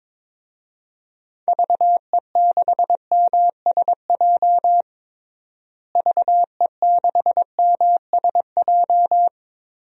Morse code, heard as a keyed tone, about 22 wpm, 700 Hz. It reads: VE6MSJ VE6MSJ
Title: DXCC-100 Call Signs - 6 Characters - Repeated 3x - 22wpm